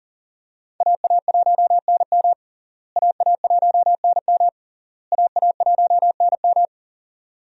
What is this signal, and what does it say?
Morse code, heard as a keyed tone, about 40 wpm, 700 Hz. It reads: AA1NM AA1NM AA1NM
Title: USA Call Signs - 5 Characters - Repeated 3x - 40wpm